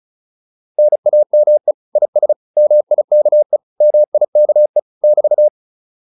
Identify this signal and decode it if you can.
Morse code, heard as a keyed tone, about 34 wpm, 600 Hz. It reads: NAME IS MIKE MIKE <BT>
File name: QSO310 35